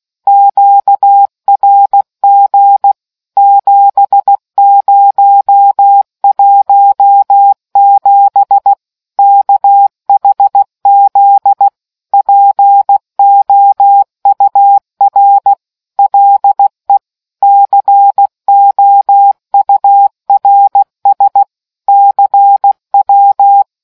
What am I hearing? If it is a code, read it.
Morse code, heard as a keyed tone, about 16 wpm, 800 Hz. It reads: QRG 7017 KHZ POUR LE COURS CW